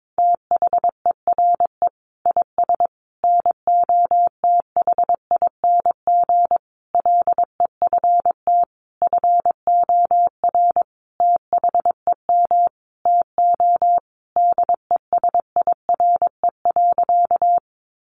Morse code, heard as a keyed tone, about 22 wpm, 700 Hz. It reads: THERE IS NOTHING LEFT FOR THEM TO DESIRE.
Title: Sentences from Top 1100 Words 22wpm